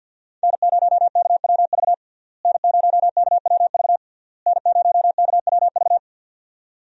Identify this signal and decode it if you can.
Morse code, heard as a keyed tone, about 47 wpm, 700 Hz. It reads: N0KWV N0KWV N0KWV
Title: USA Call Signs - 5 Characters - Repeated 3x - 50wpm